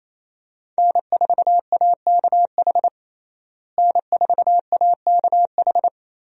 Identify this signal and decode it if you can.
Morse code, heard as a keyed tone, about 28 wpm, 700 Hz. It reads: N4AKH N4AKH
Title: USA Call Signs - 5 Characters - Repeated 3x - 28wpm